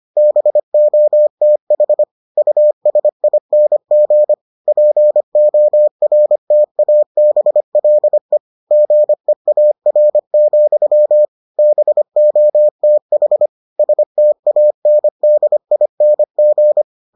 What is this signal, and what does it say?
Morse code, heard as a keyed tone, about 25 wpm, 600 Hz. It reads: BOTH USING PORTABLE GEAR, BOTH STANDING